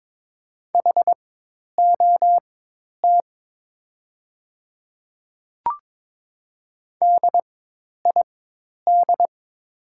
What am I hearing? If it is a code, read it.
Morse code, heard as a keyed tone, about 22 wpm, spaced as for 10 wpm, 700 Hz. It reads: HOT DID
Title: 4x Character Spacing - Top 100 Words - 22wpm